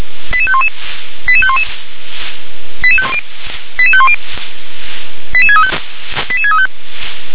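Five-tone selective call (ZVEI), heard as quick runs of five tones, tones of about 70 ms. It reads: 88511 88511 88510 88635 88635
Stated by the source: Received via SDR as a ZVEI five-tone call, address 88747.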